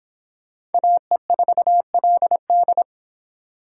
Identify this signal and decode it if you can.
Morse code, heard as a keyed tone, about 30 wpm, 700 Hz. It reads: AE4LD